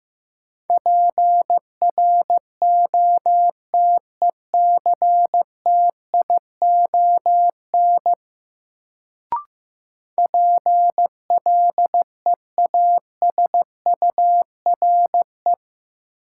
Morse code, heard as a keyed tone, about 15 wpm, 700 Hz. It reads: PROTECTION PLEASURE
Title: Sentences from Top 1600 Words - New Word Drill - 15wpm